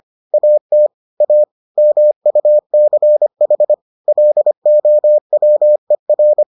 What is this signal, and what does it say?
Morse code, heard as a keyed tone, about 25 wpm, 600 Hz. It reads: AT A MUCH LOWER